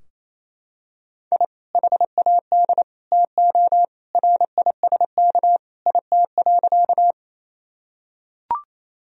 Morse code, heard as a keyed tone, about 28 wpm, 700 Hz. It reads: I HAD TO RISK IT.
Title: Sentences from Top 1000 Words - Review Part 4 - 28wpm